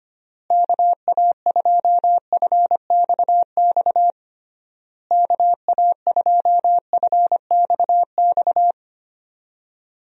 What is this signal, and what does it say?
Morse code, heard as a keyed tone, about 25 wpm, 700 Hz. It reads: KA2FXX KA2FXX
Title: USA Call Signs - 6 Characters - Repeated 3x - 25wpm